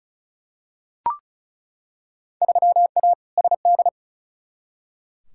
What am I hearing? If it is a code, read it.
Morse code, heard as a keyed tone, about 34 wpm, 700 Hz. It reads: 3A SD
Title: Field Day Exchanges - 35wpm